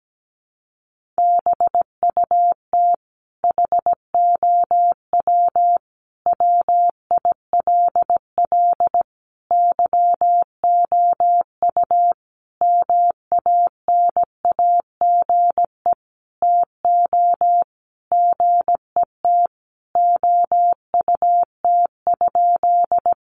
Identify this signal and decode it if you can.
Morse code, heard as a keyed tone, about 17 wpm, 700 Hz. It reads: BUT HOW WILL YOU MANAGE TO GET OUT?
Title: Sentences from Top 1400 Words 17wpm